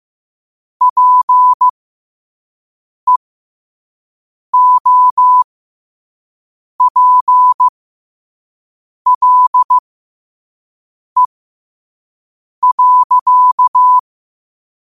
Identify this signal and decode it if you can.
Morse code, heard as a keyed tone, about 15 wpm, spaced as for 5 wpm, 1000 Hz. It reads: PEOPLE.
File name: qotdfiles / 5wpm